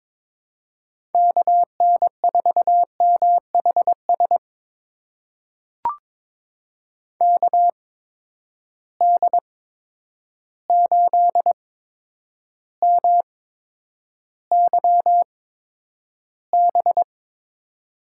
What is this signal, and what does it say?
Morse code, heard as a keyed tone, about 22 wpm, spaced as for 9 wpm, 700 Hz. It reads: KN4MHS KD8MYB
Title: USA Call Signs - 6 Characters - 8x Character Spacing - 22wpm